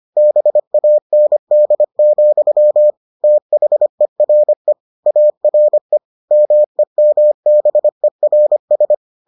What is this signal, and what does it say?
Morse code, heard as a keyed tone, about 25 wpm, 600 Hz. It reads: BAND, THERE ARE MEMBERS